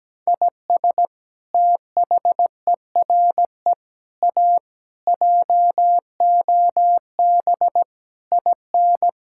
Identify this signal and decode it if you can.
Morse code, heard as a keyed tone, about 17 wpm, 700 Hz. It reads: IS THERE A JOB IN
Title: Sentences from Top 1600 Words 17wpm